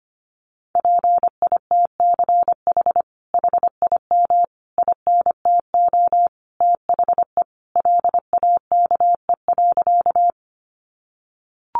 Morse code, heard as a keyed tone, about 25 wpm, 700 Hz. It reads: PITCH HIM INTO THE LAKE.
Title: Sentences from Top 1300 Words 25wpm